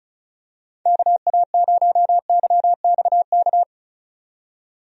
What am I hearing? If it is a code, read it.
Morse code, heard as a keyed tone, about 34 wpm, 700 Hz. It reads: KA0YXK
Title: USA Call Signs - 6 Characters - 35wpm